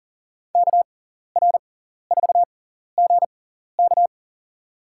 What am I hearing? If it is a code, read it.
Morse code, heard as a keyed tone, about 40 wpm, spaced as for 14 wpm, 700 Hz. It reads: KR4GK
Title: USA Call Signs - 5 Characters - 6x Character Spacing - 40wpm